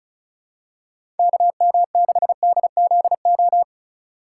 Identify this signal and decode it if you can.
Morse code, heard as a keyed tone, about 36 wpm, 700 Hz. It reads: KM6DZO
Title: USA Call Signs - 6 Characters - 4x Character Spacing - 35wpm